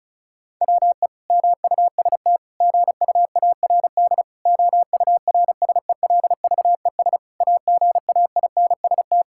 Morse code, heard as a keyed tone, about 36 wpm, 700 Hz. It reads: WE MUST GUARD OURSELVES AGAINST
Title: Sentences from Top 1600 Words 35wpm